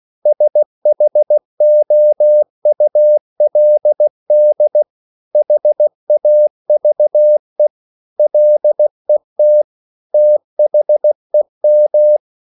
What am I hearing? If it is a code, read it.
Morse code, heard as a keyed tone, about 16 wpm, 600 Hz. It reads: SHOULD HAVE LET THEM